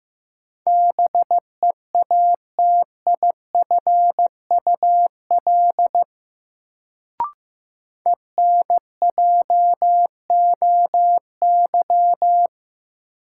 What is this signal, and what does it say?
Morse code, heard as a keyed tone, about 15 wpm, 700 Hz. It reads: BEATIFUL ENJOY
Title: Sentences from Top 1300 Words - New Word Drill Alt - 15wpm